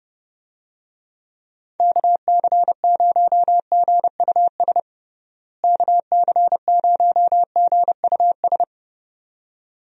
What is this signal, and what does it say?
Morse code, heard as a keyed tone, about 28 wpm, 700 Hz. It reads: KC0GUS KC0GUS
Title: USA Call Signs - 6 Characters - Repeated 2x - 30wpm